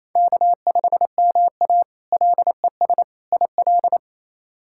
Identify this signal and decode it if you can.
Morse code, heard as a keyed tone, about 27 wpm, 700 Hz. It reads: K5MA LES IL
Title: K1USN SST QSO Practice - 28wpm